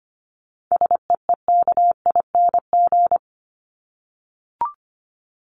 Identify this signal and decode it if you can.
Morse code, heard as a keyed tone, about 25 wpm, 700 Hz. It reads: SEEKING
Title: ING Suffix - Top 100 Words with Base Words 25wpm